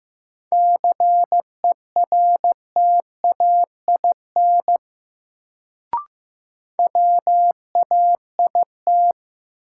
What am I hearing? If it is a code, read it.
Morse code, heard as a keyed tone, about 15 wpm, 700 Hz. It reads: CERTAIN WAIT
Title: Sentences from Top 500 Words - New Word Drill - 15wpm